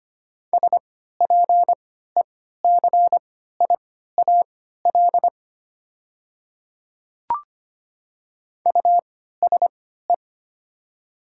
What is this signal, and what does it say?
Morse code, heard as a keyed tone, about 25 wpm, spaced as for 14 wpm, 700 Hz. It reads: SPECIAL USE
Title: 3x Character Spacing - Top 500 Words - 25wpm